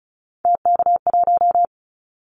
Morse code, heard as a keyed tone, about 34 wpm, 700 Hz. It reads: TK1
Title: Callsign Prefixes for Top 100 DXCC - 35wpm